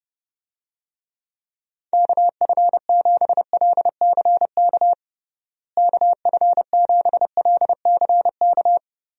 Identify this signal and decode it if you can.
Morse code, heard as a keyed tone, about 32 wpm, 700 Hz. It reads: KF7LCK KF7LCK
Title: USA Call Signs - 6 Characters - Repeated 3x - 30wpm